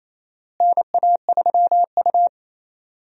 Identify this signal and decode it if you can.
Morse code, heard as a keyed tone, about 28 wpm, 700 Hz. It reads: NA3U
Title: USA Call Signs - 4 Characters - Repeated 3x - 28wpm